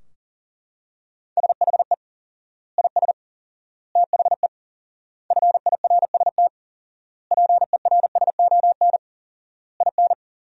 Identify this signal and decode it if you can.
Morse code, heard as a keyed tone, about 40 wpm, 700 Hz. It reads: SHE IS THE FIRST PERSON IN
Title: 4x Word-Spacing - Sentences from Top 400 Words - 40wpm